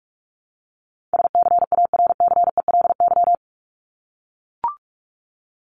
Morse code, heard as a keyed tone, about 44 wpm, 700 Hz. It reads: SCARCELY
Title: LY Suffix - Top 100 Words with Base Words 45wpm